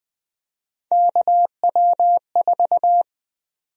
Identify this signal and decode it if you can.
Morse code, heard as a keyed tone, about 20 wpm, 700 Hz. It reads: KW4
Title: USA Call Sign Prefixes - Repeated 2x - 20wpm